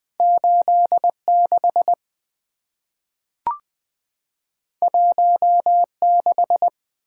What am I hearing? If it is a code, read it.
Morse code, heard as a keyed tone, about 20 wpm, 700 Hz. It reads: 86 16
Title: Two Letter-Number Combinations - 20wpm